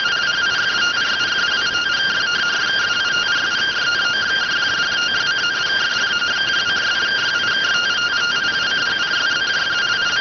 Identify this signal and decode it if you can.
Two-tone teleprinter signal, about 170 Hz shift.